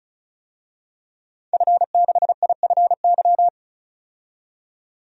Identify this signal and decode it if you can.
Morse code, heard as a keyed tone, about 36 wpm, 700 Hz. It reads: F6IFY
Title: DXCC-100 Call Signs - 5 Characters - 35wpm